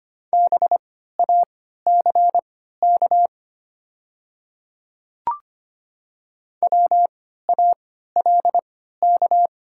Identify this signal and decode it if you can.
Morse code, heard as a keyed tone, about 25 wpm, spaced as for 14 wpm, 700 Hz. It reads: BACK WALK
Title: 3x Character Spacing - Top 300 Words - 25wpm